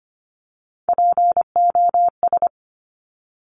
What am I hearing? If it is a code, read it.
Morse code, heard as a keyed tone, about 25 wpm, 700 Hz. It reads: POS
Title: Top Three Letter Combinations in Words 25wpm